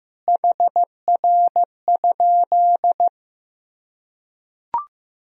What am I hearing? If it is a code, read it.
Morse code, heard as a keyed tone, about 15 wpm, 700 Hz. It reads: HR?
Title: Sentences from Top 200 Words and Abbreviations 15wpm